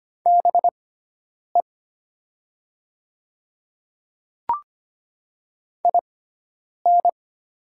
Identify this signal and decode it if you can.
Morse code, heard as a keyed tone, about 25 wpm, spaced as for 9 wpm, 690 Hz. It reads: BE IN
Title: 6x Character Spacing - Top 200 Words - 25wpm